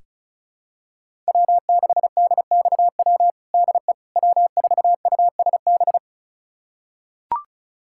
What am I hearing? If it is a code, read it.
Morse code, heard as a keyed tone, about 36 wpm, 700 Hz. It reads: W6DXW DE W4USB
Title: USA Callsign DE Callsign - 5 Characters - 35wpm